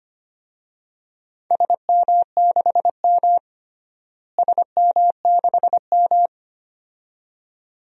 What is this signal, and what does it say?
Morse code, heard as a keyed tone, about 25 wpm, 700 Hz. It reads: SM6M SM6M
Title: DXCC-100 Call Signs - 4 Characters - Repeated 2x - 25wpm